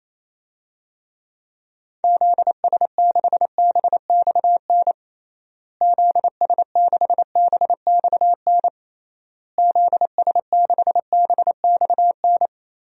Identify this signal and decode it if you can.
Morse code, heard as a keyed tone, about 28 wpm, 700 Hz. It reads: ZS6BXN ZS6BXN ZS6BXN
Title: DXCC-100 Call Signs - 6 Characters - Repeated 3x - 28wpm